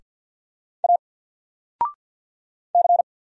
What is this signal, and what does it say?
Morse code, heard as a keyed tone, about 50 wpm, 700 Hz. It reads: A C
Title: Single Letter-Number 50wpm